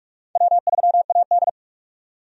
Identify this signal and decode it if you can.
Morse code, heard as a keyed tone, about 46 wpm, 700 Hz. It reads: W3AD
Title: USA Call Signs - 4 Characters - 45wpm